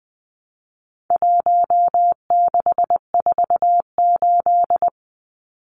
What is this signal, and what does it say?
Morse code, heard as a keyed tone, about 20 wpm, 700 Hz. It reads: 1648
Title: Numbers - 4-Digit 20wpm